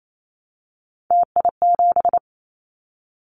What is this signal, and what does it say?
Morse code, heard as a keyed tone, about 28 wpm, 700 Hz. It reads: TI7
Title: Callsign Prefixes for Top 100 DXCC - 28wpm